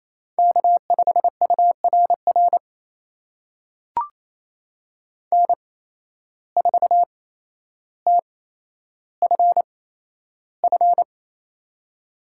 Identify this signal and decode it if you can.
Morse code, heard as a keyed tone, about 27 wpm, spaced as for 12 wpm, 700 Hz. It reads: K5URR N4TFF
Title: USA Call Signs - 5 Characters - 8x Character Spacing - 28wpm